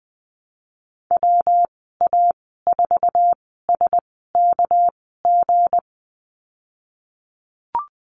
Morse code, heard as a keyed tone, about 20 wpm, spaced as for 15 wpm, 700 Hz. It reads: WA4SKG
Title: 231 Lesson 37g - Letter x - Call Signs 20wpm